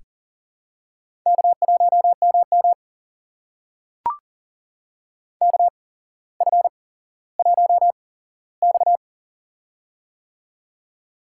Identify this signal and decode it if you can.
Morse code, heard as a keyed tone, about 39 wpm, spaced as for 17 wpm, 700 Hz. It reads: K1MM KF1X
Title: USA Call Signs - 4 Characters - 8x Character Spacing - 40wpm